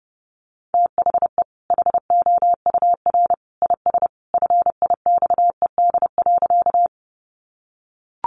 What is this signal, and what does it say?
Morse code, heard as a keyed tone, about 30 wpm, 700 Hz. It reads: THE HOUR IS FIXED.
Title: Sentences from Top 1100 Words 30wpm